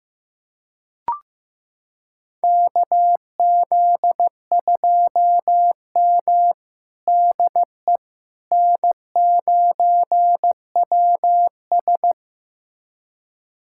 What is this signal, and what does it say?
Morse code, heard as a keyed tone, about 15 wpm, 700 Hz. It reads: KZ2M DE N9WS
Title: USA Callsign DE Callsign - 4 Characters - 15wpm